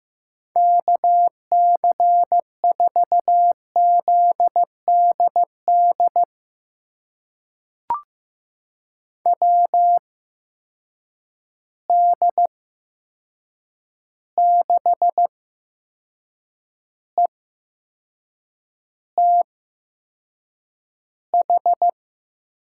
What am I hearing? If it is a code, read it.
Morse code, heard as a keyed tone, about 15 wpm, spaced as for 6 wpm, 700 Hz. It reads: KC4ZDD WD6ETH
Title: USA Call Signs - 6 Characters - 8x Character Spacing - 15wpm